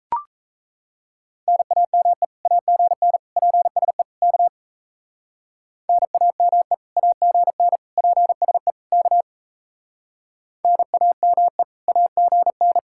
Morse code, heard as a keyed tone, about 34 wpm, 700 Hz. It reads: NAME AGN PSE K NAME AGN PSE K NAME AGN
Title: Speed-Racing Common QSO Phrases - 28wpm